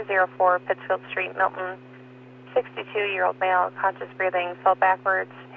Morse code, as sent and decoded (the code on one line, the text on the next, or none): none